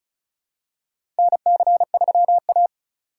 Morse code, heard as a keyed tone, about 36 wpm, 700 Hz. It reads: NC3A
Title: USA Call Signs - 4 Characters - 3x Character Spacing - 35wpm